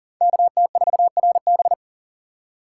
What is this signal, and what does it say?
Morse code, heard as a keyed tone, about 52 wpm, 700 Hz. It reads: KT4RB